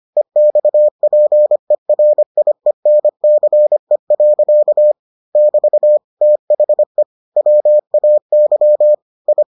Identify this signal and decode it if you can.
Morse code, heard as a keyed tone, about 25 wpm, 600 Hz. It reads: EXPERIENCE. <BT> THE WAY I